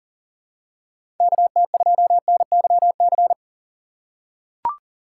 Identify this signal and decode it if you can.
Morse code, heard as a keyed tone, about 40 wpm, 700 Hz. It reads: KT2NYC
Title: USA Call Signs - 6 Characters - 6x Character Spacing - 40wpm